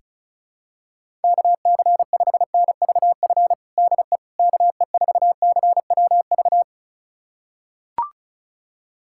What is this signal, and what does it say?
Morse code, heard as a keyed tone, about 36 wpm, 700 Hz. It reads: KC5NVF DE KE4CWV
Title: USA Callsign DE Callsign 35wpm